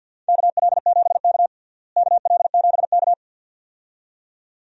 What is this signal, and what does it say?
Morse code, heard as a keyed tone, about 53 wpm, 700 Hz. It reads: KL7K KL7K
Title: DXCC-100 Call Signs - 4 Characters - Repeated 2x - 50wpm